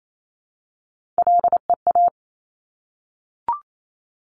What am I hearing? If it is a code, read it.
Morse code, heard as a keyed tone, about 28 wpm, 700 Hz. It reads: LEA
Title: Top Three Letter Combinations in Words 28wpm